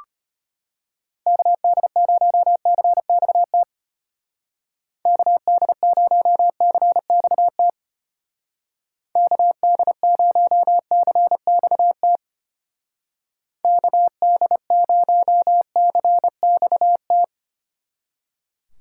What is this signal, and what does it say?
Morse code, heard as a keyed tone, about 31 wpm, 700 Hz. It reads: KD0CXT KD0CXT KD0CXT KD0CXT
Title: Speed Racing - USA 2x3 Call Signs 25wpm